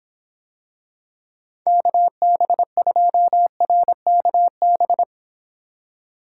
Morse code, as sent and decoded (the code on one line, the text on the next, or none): -.- -... ..--- .-. -.- -...
KB2RKB